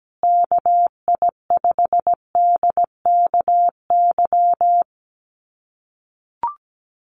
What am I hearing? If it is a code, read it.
Morse code, heard as a keyed tone, about 17 wpm, 700 Hz. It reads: KI5DKY
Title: USA Call Signs Repeated 2x 17wpm